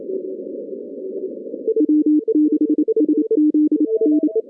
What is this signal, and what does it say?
Morse code, heard as a keyed tone, about 28 wpm, 310 Hz. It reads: W6SZN
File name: 2015 cq ww cw / 2015 cqwww k5tr - 0-18